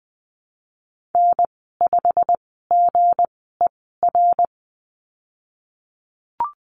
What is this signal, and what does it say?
Morse code, heard as a keyed tone, about 19 wpm, spaced as for 14 wpm, 700 Hz. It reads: N5GER
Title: 173 Lesson 28g - Number 9 - Call Signs 20wpm